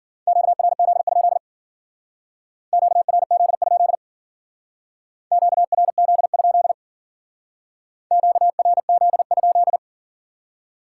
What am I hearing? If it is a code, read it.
Morse code, heard as a keyed tone, about 52 wpm, 700 Hz. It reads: QRZ? QRZ? QRZ? QRZ?